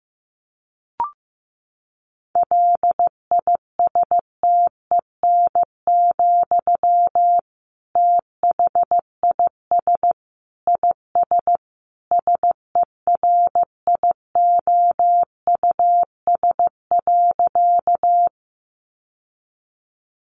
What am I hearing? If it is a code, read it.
Morse code, heard as a keyed tone, about 15 wpm, 700 Hz. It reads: LISTEN, THIS IS SERIOUS.
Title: Sentences from Top 1100 Words 15wpm